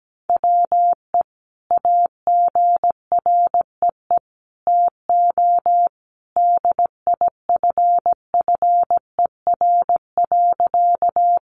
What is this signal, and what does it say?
Morse code, heard as a keyed tone, about 17 wpm, 700 Hz. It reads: WE AGREE TO DIFFER.
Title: Sentences from Top 1100 Words 17wpm